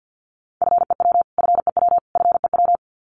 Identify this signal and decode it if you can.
Morse code, heard as a keyed tone, about 49 wpm, 700 Hz. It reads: FEW FEW FEW
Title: Top 300 Words Repeated 50wpm